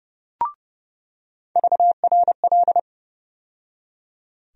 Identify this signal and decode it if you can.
Morse code, heard as a keyed tone, about 30 wpm, 700 Hz. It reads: VRL
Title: USA Call Sign Postfixes - 3 Letters 30wpm